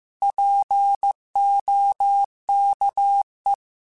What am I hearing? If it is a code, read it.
Morse code, heard as a keyed tone, about 15 wpm, 800 Hz. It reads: POKE
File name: Alphabetical / Alphabetical mp3s / Ps @13